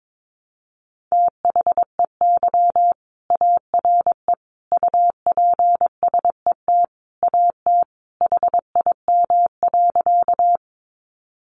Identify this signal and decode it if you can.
Morse code, heard as a keyed tone, about 22 wpm, 700 Hz. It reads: THEY ARE UPSET AT HIM.
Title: Sentences from Top 800 Words 22wpm